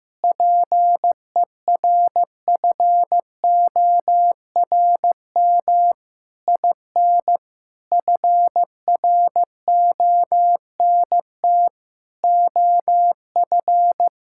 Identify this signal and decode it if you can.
Morse code, heard as a keyed tone, about 15 wpm, 700 Hz. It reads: PERFORM IN FRONT OF
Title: Sentences from Top 1000 Words - Review Part 5 - 15wpm